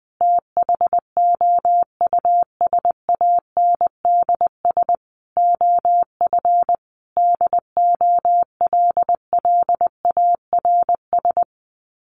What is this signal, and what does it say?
Morse code, heard as a keyed tone, about 20 wpm, 700 Hz. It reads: THOUSANDS OF DOLLARS
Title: Sets of 5 Words Encore 20wpm